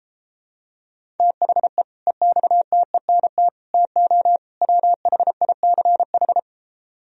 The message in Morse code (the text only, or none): THE EXTENT TO WHICH